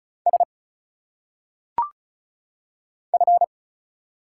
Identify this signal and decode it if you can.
Morse code, heard as a keyed tone, about 34 wpm, 700 Hz. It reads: S F
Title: Single Letters - Phonetic Pronunciation 35wpm